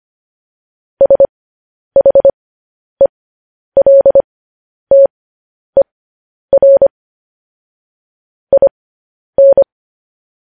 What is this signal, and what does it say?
Morse code, heard as a keyed tone, about 25 wpm, spaced as for 10 wpm, 550 Hz. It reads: SHELTER IN
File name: podcast / News headlines-1x-25x10